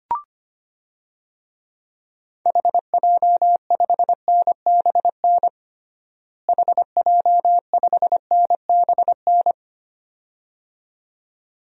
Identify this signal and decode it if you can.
Morse code, heard as a keyed tone, about 25 wpm, 700 Hz. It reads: HJ5NBN HJ5NBN
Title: DXCC-100 Call Signs - 6 Characters - Repeated 2x - 25wpm